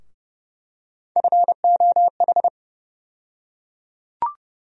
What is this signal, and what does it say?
Morse code, heard as a keyed tone, about 30 wpm, 700 Hz. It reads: FOH
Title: USA Call Sign Postfixes - 3 Letters 30wpm